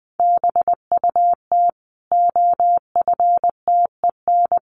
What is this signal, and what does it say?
Morse code, heard as a keyed tone, about 20 wpm, 700 Hz. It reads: BUT OFTEN
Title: Sentences from Top 1400 Words 20wpm